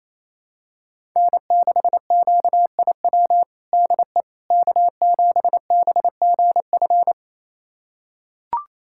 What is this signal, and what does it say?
Morse code, heard as a keyed tone, about 28 wpm, 700 Hz. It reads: N6QIW DE K7BGF
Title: USA Callsign DE Callsign - 5 Characters - 28wpm